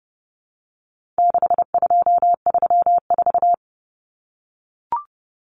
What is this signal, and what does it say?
Morse code, heard as a keyed tone, about 30 wpm, 700 Hz. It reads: 6234
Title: Common Cut Numbers - 4-Digit 30wpm